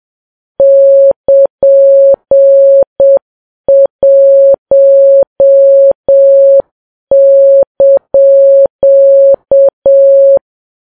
Morse code, heard as a keyed tone, about 7 wpm, 550 Hz. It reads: <KN>1)
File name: Files Web tls / Morse on9fv / morse7kps